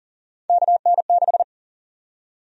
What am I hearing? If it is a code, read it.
Morse code, heard as a keyed tone, about 40 wpm, 700 Hz. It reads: KN6
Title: USA Call Sign Prefixes - 40wpm